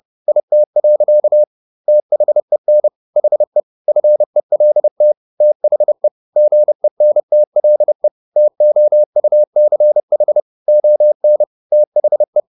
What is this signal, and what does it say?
Morse code, heard as a keyed tone, about 30 wpm, 600 Hz. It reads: IT. THEN HE FELT THE GENTLE TOUCH ON THE
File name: the old man and the sea / the old man and the sea 1 30wpm